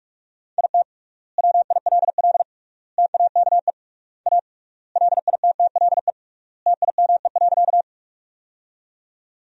Morse code, heard as a keyed tone, about 41 wpm, 700 Hz. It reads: IT WILL TAKE A LITTLE TIME.
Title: 3x Word-Spacing - Sentences from Top 200 Words - 45wpm